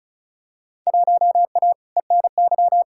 Morse code, heard as a keyed tone, about 34 wpm, 700 Hz. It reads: 1A ENY
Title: Field Day Exchanges - 35wpm